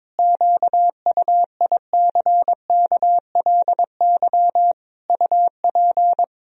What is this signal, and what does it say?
Morse code, heard as a keyed tone, about 22 wpm, 700 Hz. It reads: QUICKLY UP
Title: Sentences from Top 1000 Words - Review Part 2 - 22wpm